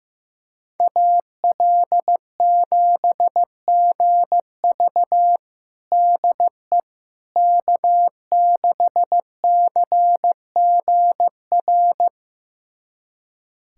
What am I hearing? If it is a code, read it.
Morse code, heard as a keyed tone, about 15 wpm, 700 Hz. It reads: AL7GV DE K6CGR
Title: USA Callsign DE Callsign - 5 Characters - 15wpm